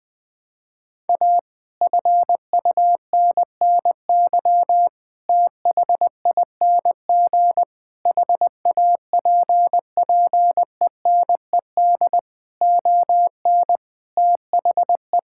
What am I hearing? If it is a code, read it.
Morse code, heard as a keyed tone, about 20 wpm, 700 Hz. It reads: A FUNNY THING HAPPENED ON THE
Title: Sets of 8 Words - 20wpm